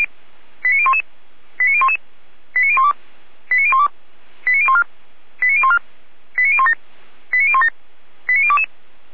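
Five-tone selective call (ZVEI), heard as quick runs of five tones, tones of about 70 ms. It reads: 89011 89011 89012 89012 89014 89014 89017 89017 89022
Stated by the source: ZVEI five-tone call 89010 received on an SDR, archived by feuerwehr-orlamuende.org fire brigade.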